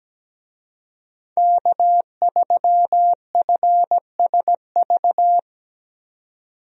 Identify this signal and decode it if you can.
Morse code, heard as a keyed tone, about 17 wpm, 700 Hz. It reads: K3FSV
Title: USA Call Signs - 5 Characters - 17wpm